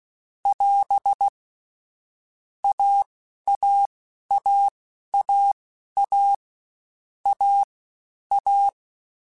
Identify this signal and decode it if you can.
Morse code, heard as a keyed tone, about 16 wpm, spaced as for 11 wpm, 800 Hz. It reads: <AS> AAAAA A A
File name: Difficult Combos / Difficult Combos-mp3s / AWJ1 @15